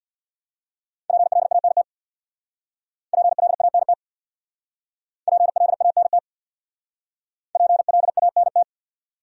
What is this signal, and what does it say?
Morse code, heard as a keyed tone, about 66 wpm, 700 Hz. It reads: PLANT PLANT PLANT PLANT